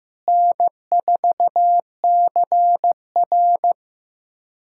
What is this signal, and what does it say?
Morse code, heard as a keyed tone, about 15 wpm, 700 Hz. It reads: N4CR
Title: USA Callsign DE Callsign - 4 Characters - 15wpm